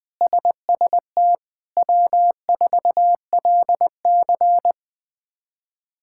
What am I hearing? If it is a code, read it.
Morse code, heard as a keyed tone, about 20 wpm, 700 Hz. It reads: SST W4LC
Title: K1USN SST QSO Practice - 20wpm